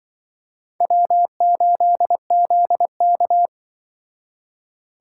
Morse code, heard as a keyed tone, about 23 wpm, 700 Hz. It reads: W8ZK
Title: Speed Racing - USA Call Signs 20wpm